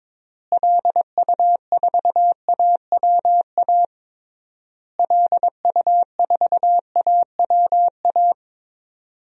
Morse code, heard as a keyed tone, about 22 wpm, 700 Hz. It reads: LU4AWA LU4AWA
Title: DXCC-100 Call Signs - 6 Characters - Repeated 3x - 22wpm